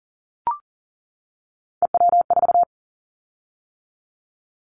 Morse code, heard as a keyed tone, about 40 wpm, 700 Hz. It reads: EW4
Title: Callsign Prefixes for Top 50 DXCC - 40wpm